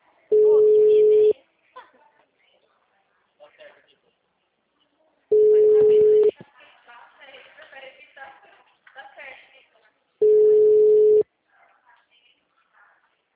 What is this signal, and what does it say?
Ringback tone (Europe, 425 Hz), with speech.
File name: out- -2009- - - .24342